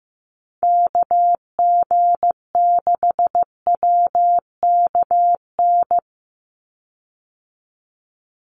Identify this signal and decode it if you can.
Morse code, heard as a keyed tone, about 15 wpm, 700 Hz. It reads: KG6WKN
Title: USA Call Signs 15wpm